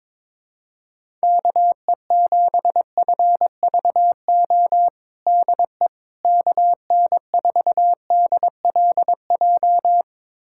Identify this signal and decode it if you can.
Morse code, heard as a keyed tone, about 21 wpm, 700 Hz. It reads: KE7FVO DE KN4DLJ
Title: USA Callsign DE Callsign - 6 Characters - 22wpm